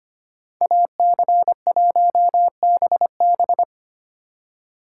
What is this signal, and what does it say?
Morse code, heard as a keyed tone, about 31 wpm, 700 Hz. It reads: AC1BB
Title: Speed Racing - USA 2x2 Call Signs 25wpm